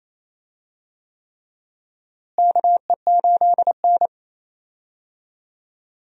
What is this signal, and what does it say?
Morse code, heard as a keyed tone, about 28 wpm, 700 Hz. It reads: KE8N
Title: USA Call Signs - 4 Characters - Repeated 3x - 28wpm